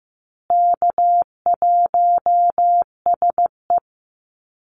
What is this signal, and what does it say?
Morse code, heard as a keyed tone, about 15 wpm, 700 Hz. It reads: K1SE
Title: Easy Call Signs Repeated 3x - 15wpm